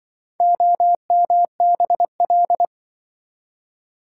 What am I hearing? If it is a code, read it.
Morse code, heard as a keyed tone, about 24 wpm, 700 Hz. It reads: OMBL
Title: Speed-Racing - Four Letter-Number Combinations - 20wpm